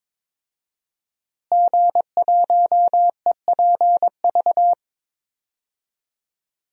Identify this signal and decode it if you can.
Morse code, heard as a keyed tone, about 22 wpm, 700 Hz. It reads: G1EPV
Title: DXCC-100 Call Signs - 5 Characters - Repeated 2x - 22wpm